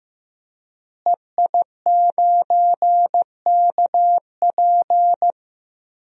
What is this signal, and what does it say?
Morse code, heard as a keyed tone, about 15 wpm, 700 Hz. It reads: EI9KP
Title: DXCC-100 Call Signs - 5 Characters - Repeated 2x - 15wpm